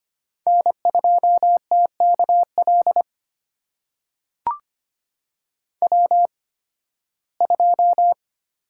Morse code, heard as a keyed tone, about 25 wpm, spaced as for 16 wpm, 700 Hz. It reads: N2TKL W2
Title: USA Call Signs - 5 Characters - 8x Character Spacing - 25wpm